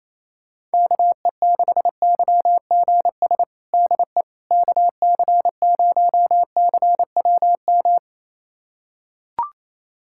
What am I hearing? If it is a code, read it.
Morse code, heard as a keyed tone, about 28 wpm, 700 Hz. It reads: KE6YGS DE KC0CWM
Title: USA Callsign DE Callsign 28wpm